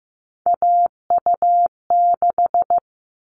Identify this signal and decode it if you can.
Morse code, heard as a keyed tone, about 15 wpm, 700 Hz. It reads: AU6
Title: All Cut Numbers - 3-Digit 15wpm